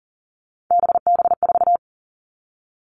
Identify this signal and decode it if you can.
Morse code, heard as a keyed tone, about 40 wpm, 700 Hz. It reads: BB4
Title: All Cut Numbers - 3-Digit 40wpm